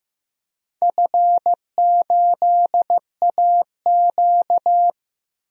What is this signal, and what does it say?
Morse code, heard as a keyed tone, about 16 wpm, 700 Hz. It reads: F8AQ
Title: DXCC-100 Call Signs - 4 Characters - 15wpm